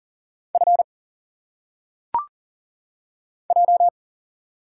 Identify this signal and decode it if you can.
Morse code, heard as a keyed tone, about 40 wpm, 700 Hz. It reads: F J
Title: Single Letters - Phonetic Pronunciation 40wpm